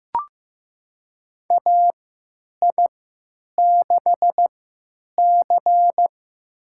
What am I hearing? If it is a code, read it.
Morse code, heard as a keyed tone, about 15 wpm, spaced as for 9 wpm, 700 Hz. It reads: AI6C
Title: USA Call Signs - 3x Character Spacing - 15wpm